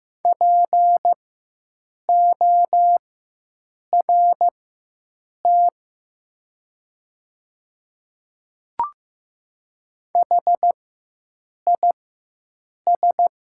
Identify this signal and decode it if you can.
Morse code, heard as a keyed tone, about 15 wpm, spaced as for 7 wpm, 700 Hz. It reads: PORT HIS
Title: 4x Character Spacing - Top 200 Words - 15wpm